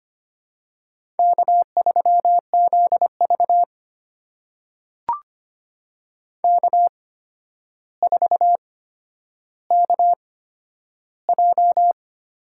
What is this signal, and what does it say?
Morse code, heard as a keyed tone, about 25 wpm, spaced as for 11 wpm, 700 Hz. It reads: K3ZV K4KJ
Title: USA Call Signs - 8x Character Spacing - 25wpm